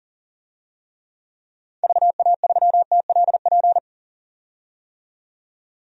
Morse code, heard as a keyed tone, about 40 wpm, 700 Hz. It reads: VA3TLP
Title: DXCC-100 Call Signs - 6 Characters - 40wpm